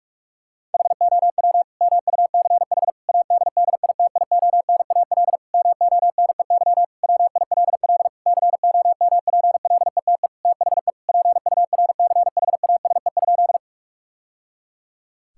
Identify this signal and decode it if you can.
Morse code, heard as a keyed tone, about 47 wpm, 700 Hz. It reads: HOW MUCH ADDITIONAL MONEY WILL COMPLETE THE PURCHASE?
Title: Sentences from Top 1000 Words - Review Part 5 - 45wpm